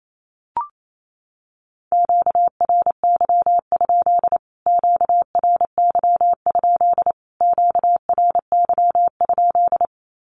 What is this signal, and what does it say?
Morse code, heard as a keyed tone, about 28 wpm, 700 Hz. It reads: QRY? QRY? QRY?
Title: Q-Codes Repeated 3x 28wpm